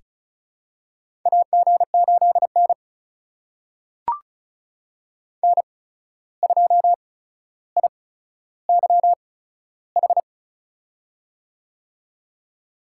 Morse code, heard as a keyed tone, about 36 wpm, spaced as for 14 wpm, 700 Hz. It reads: AG8N N2IYH
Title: USA Call Signs - 8x Character Spacing - 35wpm